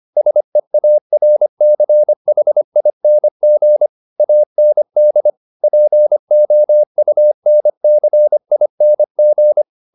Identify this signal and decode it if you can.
Morse code, heard as a keyed tone, about 25 wpm, 600 Hz. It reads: SEARCHING AND POUNCING